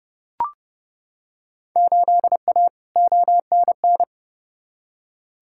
Morse code, heard as a keyed tone, about 30 wpm, 700 Hz. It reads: 8A ONN
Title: Field Day Exchanges Encore - 30wpm